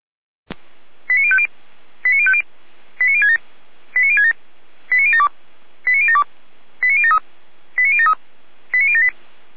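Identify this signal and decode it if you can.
Five-tone selective call (ZVEI), heard as quick runs of five tones, tones of about 70 ms. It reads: 89050 89050 89067 89067 89072 89072 89073 89073 89078
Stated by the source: ZVEI five-tone call received on an SDR, from the feuerwehr-orlamuende.org fire brigade audio archive.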